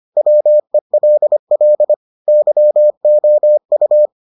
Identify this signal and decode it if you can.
Morse code, heard as a keyed tone, about 25 wpm, 600 Hz. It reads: WELL YOU